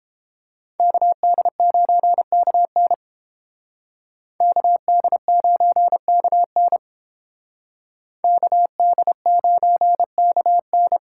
Morse code, heard as a keyed tone, about 30 wpm, 700 Hz. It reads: KD9KN KD9KN KD9KN